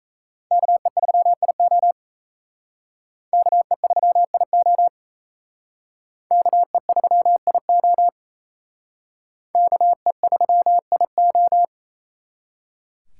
Morse code, heard as a keyed tone, about 36 wpm, 700 Hz. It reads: KE3IO KE3IO KE3IO KE3IO